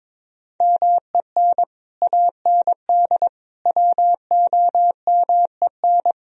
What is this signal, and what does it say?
Morse code, heard as a keyed tone, about 23 wpm, 700 Hz. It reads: MEN AND WOMEN
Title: Speed-Racing - Sets of 3 Words 22wpm